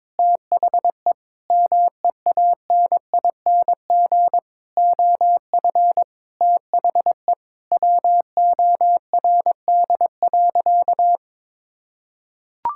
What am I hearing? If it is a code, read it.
Morse code, heard as a keyed tone, about 22 wpm, 700 Hz. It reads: THE MEANING OF THE WORD.
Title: Sentences from Top 1000 Words - Review Part 6 - 22wpm